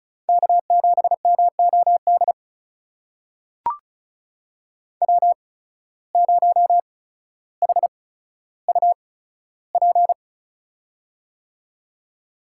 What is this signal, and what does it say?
Morse code, heard as a keyed tone, about 36 wpm, spaced as for 15 wpm, 700 Hz. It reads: K7MOD W0HUP
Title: USA Call Signs - 5 Characters - 8x Character Spacing - 35wpm